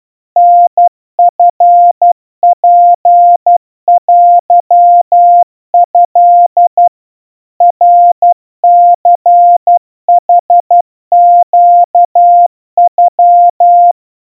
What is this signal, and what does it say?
Morse code (Japanese Wabun code), heard as a keyed tone, about 12 wpm, 700 Hz. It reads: タチツテト ナニヌネノ
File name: aiueo 40 cpm 700 Hz Crispy